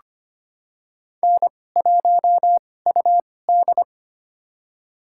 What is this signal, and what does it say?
Morse code, heard as a keyed tone, about 25 wpm, spaced as for 18 wpm, 700 Hz. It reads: N1UD
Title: USA Call Signs - 2x Character Spacing - 25wpm